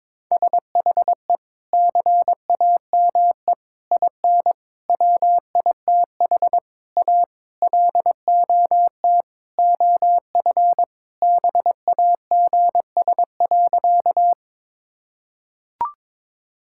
Morse code, heard as a keyed tone, about 22 wpm, 700 Hz. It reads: SHE CAME IN WITH A LOT OF BAGS.
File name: practice / 22wpm / Sentences with 4-Word Phrases - 22wpm